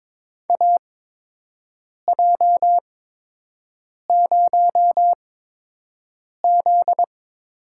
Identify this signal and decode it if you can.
Morse code, heard as a keyed tone, about 22 wpm, spaced as for 6 wpm, 700 Hz. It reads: AJ0Z
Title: USA Call Signs - 8x Character Spacing - 22wpm